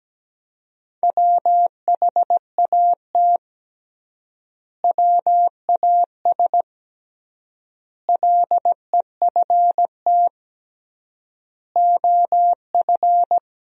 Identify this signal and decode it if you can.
Morse code, heard as a keyed tone, about 17 wpm, 700 Hz. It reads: WHAT WAS LEFT OF
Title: 3x Word-Spacing - Sentences from Top 300 Words - 17wpm